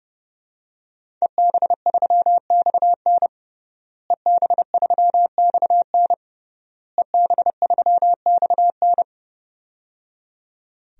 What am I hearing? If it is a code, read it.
Morse code, heard as a keyed tone, about 30 wpm, 700 Hz. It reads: EB3XN EB3XN EB3XN
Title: DXCC-100 Call Signs - 5 Characters - Repeated 3x - 30wpm